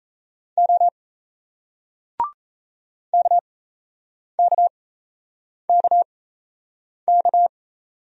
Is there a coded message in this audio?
Yes: Morse code from 0.6 to 7.5 s, heard as a keyed tone, about 37 wpm, spaced as for 8 wpm, 700 Hz. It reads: O KKKK